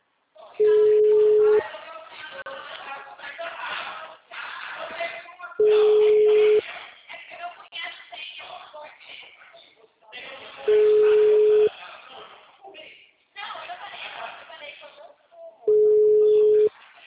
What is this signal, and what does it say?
Ringback tone (Europe, 425 Hz), with speech.